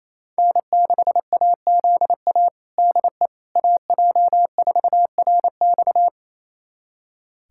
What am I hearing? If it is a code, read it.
Morse code, heard as a keyed tone, about 28 wpm, 700 Hz. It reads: N6AZA DE AJ4RX
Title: USA Callsign DE Callsign - 5 Characters - 28wpm